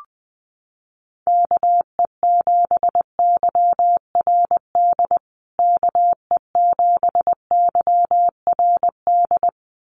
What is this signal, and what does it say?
Morse code, heard as a keyed tone, about 20 wpm, 700 Hz. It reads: KE7YRD KE7YRD
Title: USA Call Signs Repeated 2x 20wpm